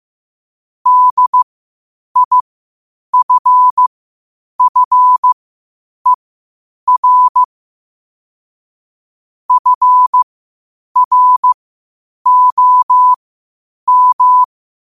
Morse code, heard as a keyed tone, about 15 wpm, spaced as for 8 wpm, 1000 Hz. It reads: DIFFER FROM
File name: qotdfiles / 8wpm